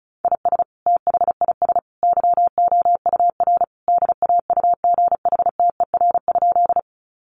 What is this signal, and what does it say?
Morse code, heard as a keyed tone, about 34 wpm, 700 Hz. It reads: IS THIS YOUR DAUGHTER?
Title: Sentences from Top 1500 Words 35wpm